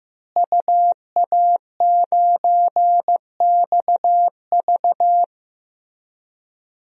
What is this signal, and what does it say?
Morse code, heard as a keyed tone, about 15 wpm, 700 Hz. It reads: UA9XV
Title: DXCC-100 Call Signs - 5 Characters - Repeated 3x - 15wpm